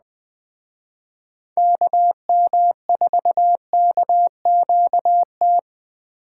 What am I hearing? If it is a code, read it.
Morse code, heard as a keyed tone, about 20 wpm, 700 Hz. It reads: KM4KQT